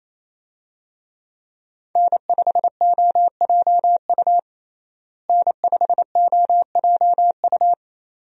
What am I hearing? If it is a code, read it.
Morse code, heard as a keyed tone, about 28 wpm, 700 Hz. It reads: N5OJU N5OJU
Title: USA Call Signs - 5 Characters - Repeated 2x - 28wpm